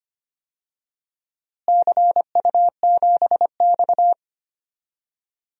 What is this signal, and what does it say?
Morse code, heard as a keyed tone, about 25 wpm, 700 Hz. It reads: CU7X
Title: DXCC-100 Call Signs - 4 Characters - Repeated 2x - 25wpm